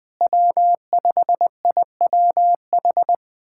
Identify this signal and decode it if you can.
Morse code, heard as a keyed tone, about 20 wpm, 700 Hz. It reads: W5IWH
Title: USA Callsign DE Callsign - 5 Characters - 20wpm